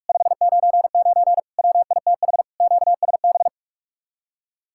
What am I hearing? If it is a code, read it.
Morse code, heard as a keyed tone, about 45 wpm, 700 Hz. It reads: 599 WITH QSB